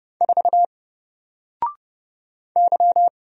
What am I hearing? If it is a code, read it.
Morse code, heard as a keyed tone, about 31 wpm, 700 Hz. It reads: 4 Y
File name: Single-Letter-Number-30wpm